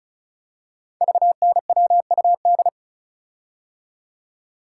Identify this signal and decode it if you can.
Morse code, heard as a keyed tone, about 36 wpm, 690 Hz. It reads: VNWUD
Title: Code Groups - 5 Letters - 35wpm